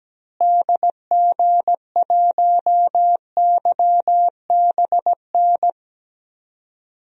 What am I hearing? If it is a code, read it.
Morse code, heard as a keyed tone, about 17 wpm, 700 Hz. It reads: DG1YBN
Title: DXCC-100 Call Signs - 6 Characters - Repeated 2x - 17wpm